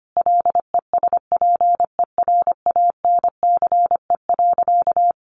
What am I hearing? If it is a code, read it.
Morse code, heard as a keyed tone, about 25 wpm, 700 Hz. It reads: LESPERANCE.
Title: A Sound Of Thunder 25wpm